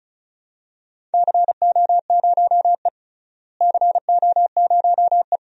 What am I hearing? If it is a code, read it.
Morse code, heard as a keyed tone, about 34 wpm, 700 Hz. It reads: CO0E CO0E
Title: DXCC-100 Call Signs - 4 Characters - Repeated 2x - 35wpm